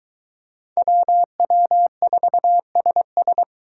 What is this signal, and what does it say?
Morse code, heard as a keyed tone, about 22 wpm, 700 Hz. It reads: WW4SS